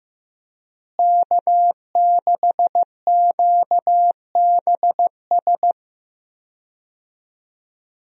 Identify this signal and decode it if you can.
Morse code, heard as a keyed tone, about 15 wpm, 700 Hz. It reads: K6QBS
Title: USA Call Signs - 5 Characters - Repeated 2x - 15wpm